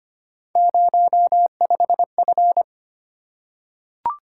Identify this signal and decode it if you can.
Morse code, heard as a keyed tone, about 25 wpm, 700 Hz. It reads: 05F
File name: Three Letter-Number Combinations - 25wpm